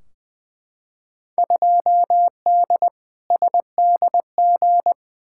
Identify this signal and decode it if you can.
Morse code, heard as a keyed tone, about 20 wpm, 700 Hz. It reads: 2D SDG
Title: Field Day Exchanges Encore - 20wpm